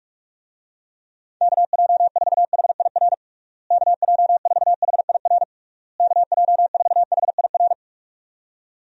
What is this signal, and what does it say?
Morse code, heard as a keyed tone, about 41 wpm, 700 Hz. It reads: KJ4HIR KJ4HIR KJ4HIR
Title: USA Call Signs - 6 Characters - Repeated 3x - 45wpm